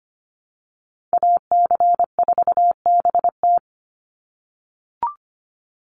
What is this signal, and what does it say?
Morse code, heard as a keyed tone, about 25 wpm, 700 Hz. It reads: AC4BT
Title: Call Signs - 25wpm